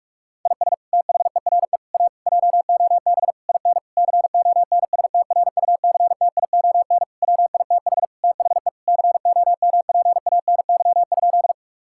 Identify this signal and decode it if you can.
Morse code, heard as a keyed tone, about 46 wpm, 700 Hz. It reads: IS THERE A JOB IN CONSTRUCTION WITH THE COMPANY?
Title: Sentences from Top 1600 Words 45wpm